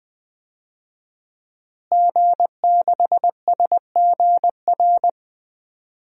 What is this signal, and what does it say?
Morse code, heard as a keyed tone, about 20 wpm, 700 Hz. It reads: G6SGR
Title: DXCC-100 Call Signs - 5 Characters - Repeated 3x - 20wpm